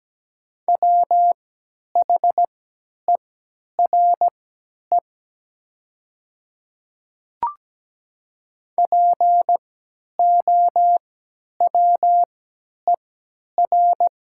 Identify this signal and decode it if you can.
Morse code, heard as a keyed tone, about 17 wpm, spaced as for 10 wpm, 690 Hz. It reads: WHERE POWER
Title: 3x Character Spacing - Top 500 Words - 17wpm